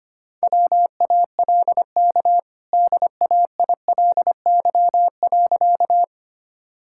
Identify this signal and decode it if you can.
Morse code, heard as a keyed tone, about 25 wpm, 700 Hz. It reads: WALK DAILY.
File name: Easy Sentences with Hints - 25wpm